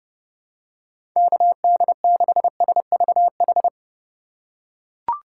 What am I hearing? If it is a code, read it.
Morse code, heard as a keyed tone, about 31 wpm, 700 Hz. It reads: KD6SVH
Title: USA Call Signs - 4x Character Spacing - 30wpm